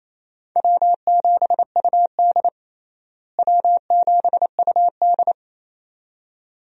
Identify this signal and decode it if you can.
Morse code, heard as a keyed tone, about 28 wpm, 700 Hz. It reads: W7UD W7UD
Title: USA Call Signs - 4 Characters - Repeated 2x - 28wpm